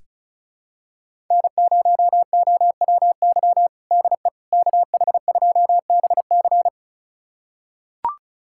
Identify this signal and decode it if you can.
Morse code, heard as a keyed tone, about 36 wpm, 700 Hz. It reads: N0OWY DE KH2BC
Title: USA Callsign DE Callsign - 5 Characters - 35wpm